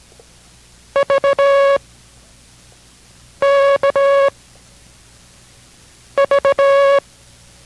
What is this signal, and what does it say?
Morse code, heard as a keyed tone, about 17 wpm, 560 Hz. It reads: V K V